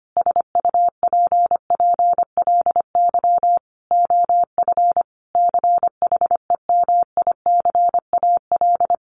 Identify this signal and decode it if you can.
Morse code, heard as a keyed tone, about 25 wpm, 700 Hz. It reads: SUPPLY OF CHEMICAL
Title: Sentences from Top 1200 Words 25wpm